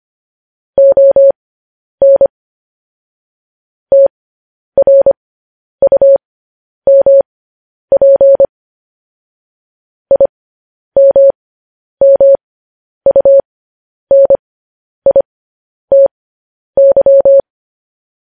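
Morse code, heard as a keyed tone, about 25 wpm, spaced as for 10 wpm, 550 Hz. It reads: ON TRUMP IMMUNITY
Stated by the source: Morse code transcription of news headlines.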